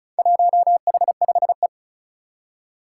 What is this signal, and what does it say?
Morse code, heard as a keyed tone, about 36 wpm, 700 Hz. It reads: 1H5E
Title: Four Challenging Letter-Number Combinations 35wpm